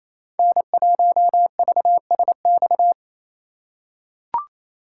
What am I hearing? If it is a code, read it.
Morse code, heard as a keyed tone, about 28 wpm, 700 Hz. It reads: N1VSX
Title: USA Call Signs - 5 Characters - 2x Character Spacing - 28wpm